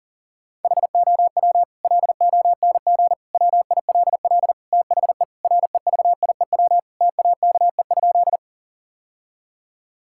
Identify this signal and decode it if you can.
Morse code, heard as a keyed tone, about 40 wpm, 700 Hz. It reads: HOW LONG WILL THE REVIEW TAKE?
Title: Sentences from Top 1000 Words - Review Part 6 - 40wpm